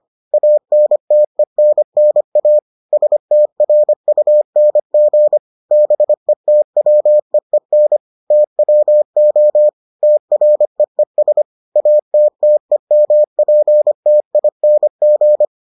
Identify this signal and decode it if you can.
Morse code, heard as a keyed tone, about 25 wpm, 600 Hz. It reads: ANTENNA STRUNG BETWEEN TWO TREES ATTEMPTING